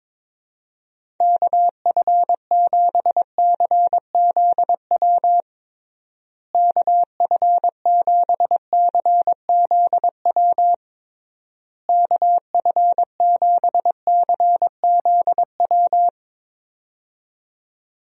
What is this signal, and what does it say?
Morse code, heard as a keyed tone, about 22 wpm, 700 Hz. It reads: KF7CZW KF7CZW KF7CZW
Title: USA Call Signs - 6 Characters - Repeated 3x - 22wpm